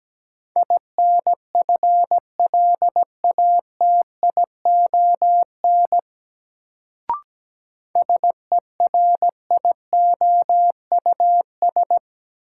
Morse code, heard as a keyed tone, about 17 wpm, 700 Hz. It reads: INFLATION SERIOUS
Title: Sentences from Top 1100 Words - New Word Drill - 17wpm